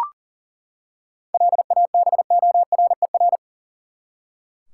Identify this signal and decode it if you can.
Morse code, heard as a keyed tone, about 40 wpm, 700 Hz. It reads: LABORER
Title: Practice QSO - Exchange 3 - More Info - 40wpm